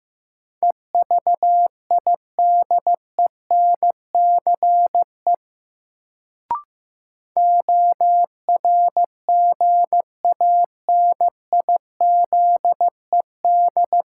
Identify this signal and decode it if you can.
Morse code, heard as a keyed tone, about 15 wpm, 700 Hz. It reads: EVIDENCE ORGANIZED
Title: Sentences from Top 1300 Words - New Word Drill Alt - 15wpm